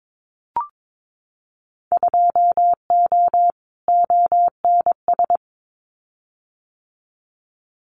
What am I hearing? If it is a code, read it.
Morse code, heard as a keyed tone, about 22 wpm, 700 Hz. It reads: 2O ONS
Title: Winter Field Day Exchanges Encore 22wpm